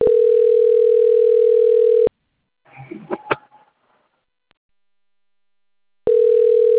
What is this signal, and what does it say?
Ringback tone (North America), 440 + 480 Hz.
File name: rg-1600-unknown- - - .70117